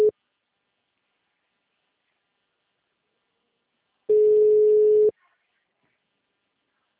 Ringback tone (Europe, 425 Hz).
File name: out- -2013- - - .4245